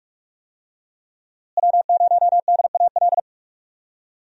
Morse code, heard as a keyed tone, about 42 wpm, 700 Hz. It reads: W0DAL
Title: USA Call Signs - 5 Characters - Repeated 3x - 45wpm